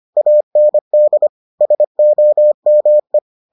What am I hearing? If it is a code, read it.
Morse code, heard as a keyed tone, about 25 wpm, 600 Hz. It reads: AND SOME